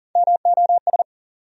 Morse code, heard as a keyed tone, about 38 wpm, 700 Hz. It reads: MOS
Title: USA Call Sign Postfixes - 3 Letters 40wpm